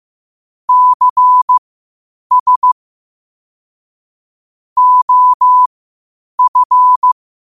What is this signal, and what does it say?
Morse code, heard as a keyed tone, about 15 wpm, spaced as for 8 wpm, 1000 Hz. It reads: CS OF
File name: qotdfiles / 8wpm